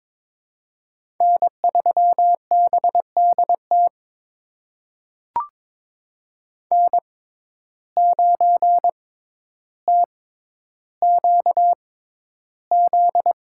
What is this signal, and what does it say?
Morse code, heard as a keyed tone, about 22 wpm, spaced as for 11 wpm, 700 Hz. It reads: N3BDT N9TQZ
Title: USA Call Signs - 5 Characters - 6x Character Spacing - 22wpm